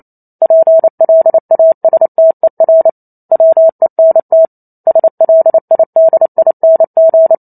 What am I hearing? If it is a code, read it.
Morse code, heard as a keyed tone, about 28 wpm, 650 Hz. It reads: PLASTER WENT SLIDING